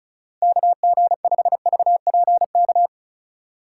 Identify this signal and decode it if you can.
Morse code, heard as a keyed tone, about 34 wpm, 700 Hz. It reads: KG5VPK
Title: USA Call Signs - 2x Character Spacing - 35wpm